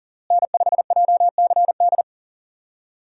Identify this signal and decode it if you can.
Morse code, heard as a keyed tone, about 40 wpm, 700 Hz. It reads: N5JCD
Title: USA Call Signs - 5 Characters - Repeated 2x - 40wpm